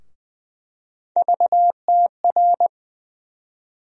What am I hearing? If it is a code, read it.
Morse code, heard as a keyed tone, about 20 wpm, 700 Hz. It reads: VTR
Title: USA Call Sign Postfixes - 3 Letters 20wpm